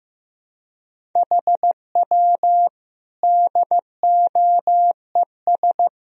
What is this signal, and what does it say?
Morse code, heard as a keyed tone, about 15 wpm, 700 Hz. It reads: HW DOES
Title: Sentences from Top 400 Words and Abbreviations 15wpm